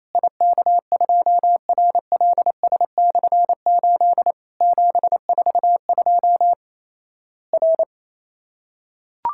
Morse code, heard as a keyed tone, about 28 wpm, 700 Hz. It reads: IK2RLS/8 742 R
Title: Worked All Europe DX Contest - QTC Exchange - 28wpm